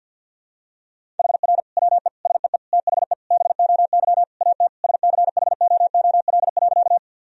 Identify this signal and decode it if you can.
Morse code, heard as a keyed tone, about 48 wpm, 700 Hz. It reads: HR WE SEE THE BOY AT SCHOOL.
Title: Sentences from Top 300 Words and Abbreviations 50wpm